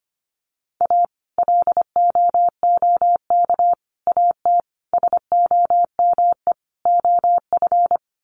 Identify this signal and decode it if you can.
Morse code, heard as a keyed tone, about 25 wpm, 700 Hz. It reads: A LOOK AT SOME OF
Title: Sets of 5 Words Encore2 25wpm